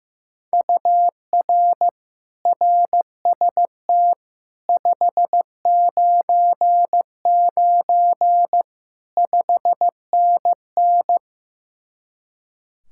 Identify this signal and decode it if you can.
Morse code, heard as a keyed tone, about 15 wpm, 700 Hz. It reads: UR RST 599 5NN
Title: Speed-Racing Common QSO Phrases Encore - 15wpm